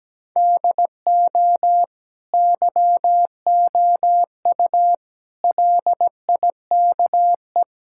Morse code, heard as a keyed tone, about 17 wpm, 700 Hz. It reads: DO YOU LIKE
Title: Sentences from Top 1000 Words - Review Part 6 - 17wpm